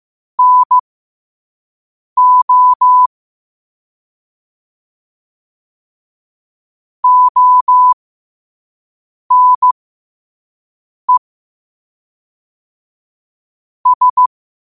Morse code, heard as a keyed tone, about 15 wpm, spaced as for 5 wpm, 1000 Hz. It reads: NO ONE S
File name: qotdfiles / 5wpm